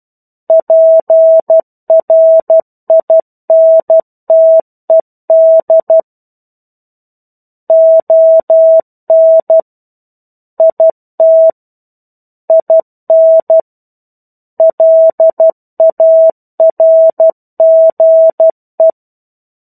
Morse code, heard as a keyed tone, about 12 wpm, 650 Hz. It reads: PRINTED ON IT IN LARGE
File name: 01-Chapter-01 12 WPM